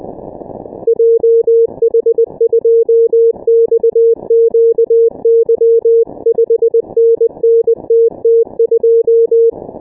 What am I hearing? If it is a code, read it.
Morse code, heard as a keyed tone, about 20 wpm, 460 Hz. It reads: JH2XQY 5NN TT2
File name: dk5ke / Contest2QSOs 100BpM